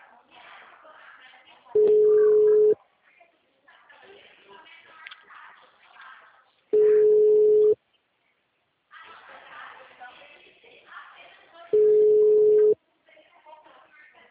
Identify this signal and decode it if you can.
Ringback tone (Europe, 425 Hz), with speech.